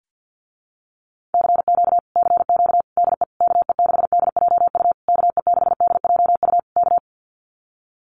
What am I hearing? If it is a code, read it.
Morse code, heard as a keyed tone, about 48 wpm, 700 Hz. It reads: CQ CQ DE KE6DJU KE6DJU K
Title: Common QSO Phrases Encore 50wpm